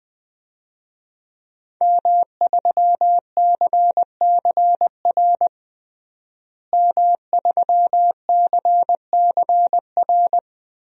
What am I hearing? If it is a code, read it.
Morse code, heard as a keyed tone, about 20 wpm, 700 Hz. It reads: M3CCR M3CCR
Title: DXCC-100 Call Signs - 5 Characters - Repeated 2x - 20wpm